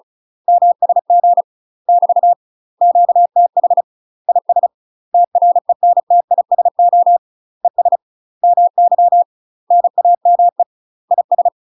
Morse code, heard as a keyed tone, about 34 wpm, 700 Hz. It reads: MSG <BT> QTH IS TRENTISO ES MY NAME IS
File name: UK MASTER QSOs TRAINING / 35WPM / UK MASTER QSOs 0037